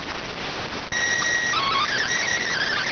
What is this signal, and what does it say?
SSTV (slow-scan television), PD 290.